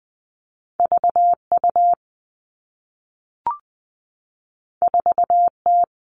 Morse code, heard as a keyed tone, about 20 wpm, 700 Hz. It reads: V U 4 T
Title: Two Challenging Letter-Number Combinations 20wpm